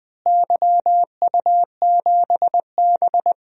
Morse code, heard as a keyed tone, about 21 wpm, 700 Hz. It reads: YU7B
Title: DXCC-100 Call Signs - 4 Characters - Repeated 3x - 20wpm